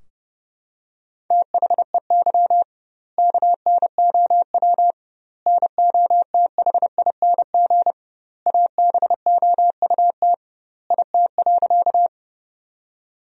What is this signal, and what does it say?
Morse code, heard as a keyed tone, about 30 wpm, 700 Hz. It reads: THEY KNOW NOTHING ABOUT IT.
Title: 2x Word-Spacing - Sentences from Top 500 Words - 30wpm